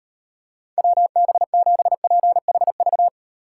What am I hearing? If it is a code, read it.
Morse code, heard as a keyed tone, about 38 wpm, 700 Hz. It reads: WB7PHV